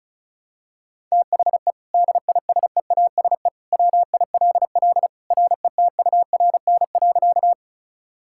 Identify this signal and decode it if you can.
Morse code, heard as a keyed tone, about 36 wpm, 700 Hz. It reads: THE DISEASE WILL RETURN.
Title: Sentences from Top 1000 Words - Review Part 6 - 35wpm